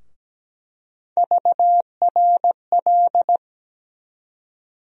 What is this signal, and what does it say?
Morse code, heard as a keyed tone, about 17 wpm, 700 Hz. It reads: VRL
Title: USA Call Sign Postfixes - 3 Letters 17wpm